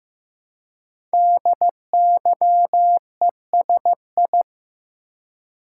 Morse code, heard as a keyed tone, about 15 wpm, 700 Hz. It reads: DYESI
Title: Code Groups - 5 Letters - 15wpm